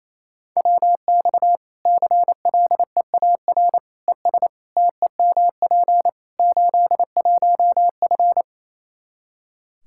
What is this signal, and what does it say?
Morse code, heard as a keyed tone, about 27 wpm, 700 Hz. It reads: WX CLEAR ES TEMP 81F
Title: Practice QSO - Exchange 2 - Info - 28wpm